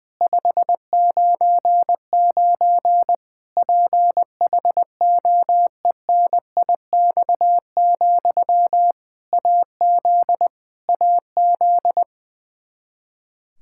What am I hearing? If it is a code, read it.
Morse code, heard as a keyed tone, about 20 wpm, 700 Hz. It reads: 599 PHOENIX, AZ AZ
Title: K1USN SST QSO Practice 20wpm